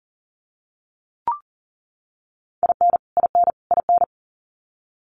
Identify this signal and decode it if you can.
Morse code, heard as a keyed tone, about 40 wpm, 700 Hz. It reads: IN IN IN
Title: US States Repeated 3x 40wpm